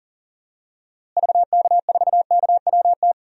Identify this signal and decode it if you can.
Morse code, heard as a keyed tone, about 38 wpm, 700 Hz. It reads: VK4KWT
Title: DXCC-100 Call Signs - 6 Characters - Repeated 2x - 40wpm